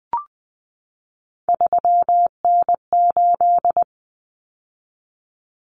Morse code, heard as a keyed tone, about 20 wpm, 700 Hz. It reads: 3N8
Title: Common Cut Numbers - 3-Digit 20wpm